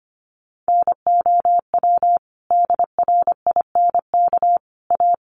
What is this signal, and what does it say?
Morse code, heard as a keyed tone, about 25 wpm, 700 Hz. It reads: NOW DRINK A
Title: Sentences from Top 1500 Words 25wpm